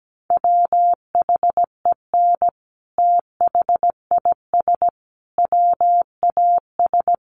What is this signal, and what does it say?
Morse code, heard as a keyed tone, about 17 wpm, 700 Hz. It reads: WHEN THIS WAS
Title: Sentences from Top 900 Words 17wpm